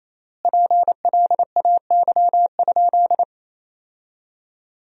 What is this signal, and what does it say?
Morse code, heard as a keyed tone, about 28 wpm, 700 Hz. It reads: PLAY?
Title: 3x Word-Spacing - Sentences from Top 400 Words - 28wpm